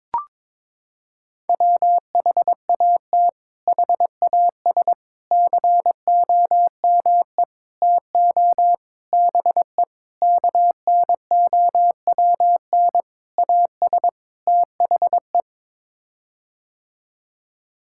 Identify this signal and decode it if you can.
Morse code, heard as a keyed tone, about 21 wpm, 700 Hz. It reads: WHAT HAS COME TO BE KNOWN AS THE
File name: practice / 22wpm / Sets of 8 Words - 22wpm